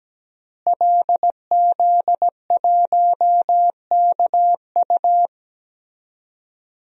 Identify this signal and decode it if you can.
Morse code, heard as a keyed tone, about 17 wpm, 700 Hz. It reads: LZ1KU
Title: DXCC-100 Call Signs - 5 Characters - Repeated 3x - 17wpm